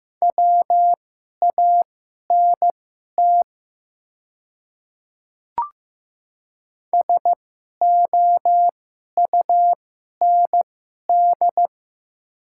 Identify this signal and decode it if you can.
Morse code, heard as a keyed tone, about 15 wpm, spaced as for 11 wpm, 700 Hz. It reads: WANT SOUND
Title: 2x Character Spacing - Top 300 Words - 15wpm